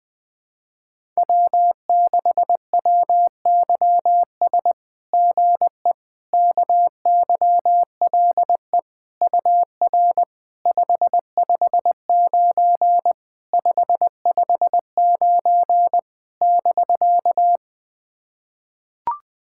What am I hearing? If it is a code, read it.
Morse code, heard as a keyed tone, about 20 wpm, 700 Hz. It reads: W6WYS GE KYLE UR 559 559 <BK>
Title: SOTA QSO Practice 20wpm